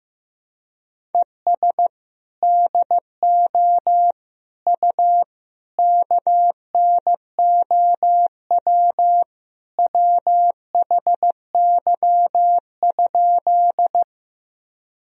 Morse code, heard as a keyed tone, about 15 wpm, 700 Hz. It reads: ES DO U KNOW WHY?
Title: Sentences from Top 200 Words and Abbreviations 15wpm